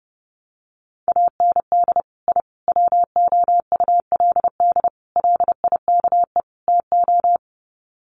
Morse code, heard as a keyed tone, about 30 wpm, 700 Hz. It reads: AND I WOULD LIKE TO
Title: Sets of 5 Words Encore 30wpm